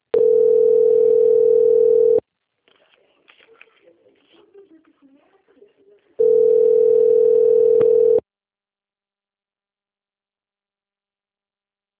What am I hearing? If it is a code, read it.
Ringback tone (North America), 440 + 480 Hz.
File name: rg-1600-unknown- - - .57814